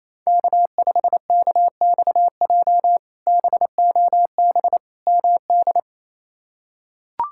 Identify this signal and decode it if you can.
Morse code, heard as a keyed tone, about 28 wpm, 700 Hz. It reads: K5KXJ BOB MD
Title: K1USN SST QSO Practice - 28wpm